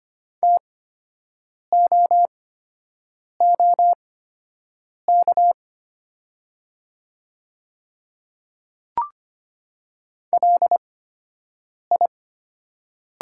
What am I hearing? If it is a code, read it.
Morse code, heard as a keyed tone, about 25 wpm, spaced as for 7 wpm, 700 Hz. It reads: TOOK LI
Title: 8x Character Spacing - Top 300 Words - 25wpm